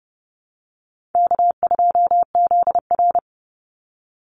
Morse code, heard as a keyed tone, about 30 wpm, 700 Hz. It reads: K2ZR
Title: Easy Call Signs - 30wpm